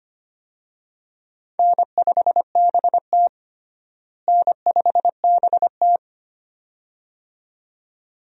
Morse code, heard as a keyed tone, about 25 wpm, 700 Hz. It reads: N5BT N5BT
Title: USA Call Signs - 4 Characters - Repeated 2x - 25wpm